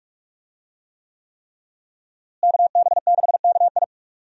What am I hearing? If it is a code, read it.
Morse code, heard as a keyed tone, about 47 wpm, 700 Hz. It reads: KB6KI
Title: USA Call Signs - 5 Characters - Repeated 3x - 45wpm